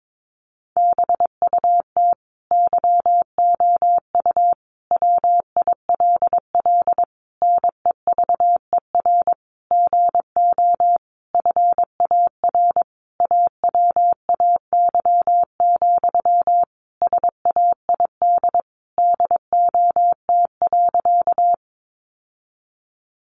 Morse code, heard as a keyed tone, about 22 wpm, 700 Hz. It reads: BUT YOU WILL NEVER GO FAR AWAY, SAID DOT.
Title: Sentences from Top 1500 Words 22wpm